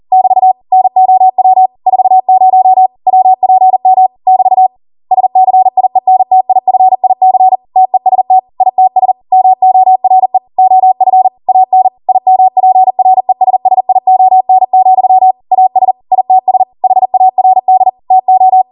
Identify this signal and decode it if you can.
Morse code, heard as a keyed tone, about 40 wpm, 750 Hz. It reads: <BT> NOW 40 WPM <BT> SCIENTIFIC TEST ITS MORE OF AN IMPRESSION, AS ITS HARD TO